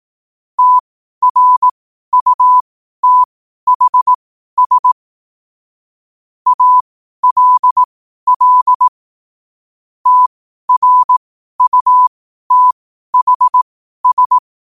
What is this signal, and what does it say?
Morse code, heard as a keyed tone, about 18 wpm, spaced as for 12 wpm, 1000 Hz. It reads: TRUTHS ALL TRUTHS
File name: qotdfiles / 12wpm